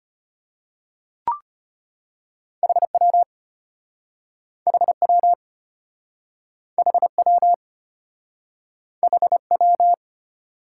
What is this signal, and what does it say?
Morse code, heard as a keyed tone, about 33 wpm, 700 Hz. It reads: HW HW HW HW